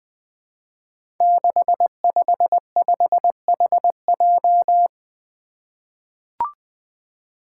Morse code, heard as a keyed tone, about 20 wpm, 700 Hz. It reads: 655HJ
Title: Five Challenging Letter-Number Combinations 20wpm